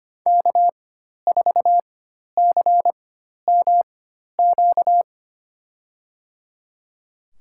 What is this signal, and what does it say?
Morse code, heard as a keyed tone, about 25 wpm, spaced as for 12 wpm, 700 Hz. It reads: K4CMQ
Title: USA Call Signs - 5 Characters - 4x Character Spacing - 25wpm